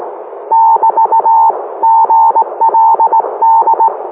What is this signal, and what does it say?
Morse code, heard as a keyed tone, about 18 wpm, 900 Hz. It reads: <BT> GLD